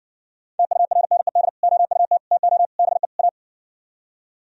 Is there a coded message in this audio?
Yes: Morse code from 0.6 to 3.3 s, heard as a keyed tone, about 69 wpm, 700 Hz. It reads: TURNED OUT TO BE A